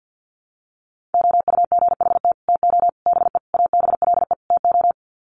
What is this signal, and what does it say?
Morse code, heard as a keyed tone, about 50 wpm, 700 Hz. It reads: OUGHT TO BE ABLE TO
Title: Sets of 5 Words Encore2 50wpm